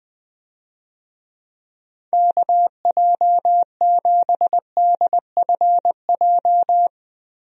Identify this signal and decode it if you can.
Morse code, heard as a keyed tone, about 20 wpm, 700 Hz. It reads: KJ7DFJ
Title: USA Call Signs - 6 Characters - Repeated 2x - 20wpm